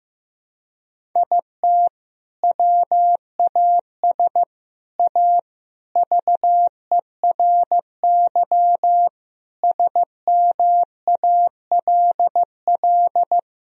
Morse code, heard as a keyed tone, about 15 wpm, 700 Hz. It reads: IT WAS A VERY SMALL
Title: Sentences from Top 1000 Words - Review Part 1 - 15wpm